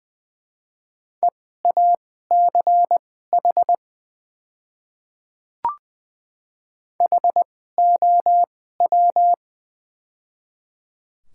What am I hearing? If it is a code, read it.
Morse code, heard as a keyed tone, about 20 wpm, spaced as for 14 wpm, 700 Hz. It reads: EACH HOW
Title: 2x Character Spacing - Top 200 Words - 20wpm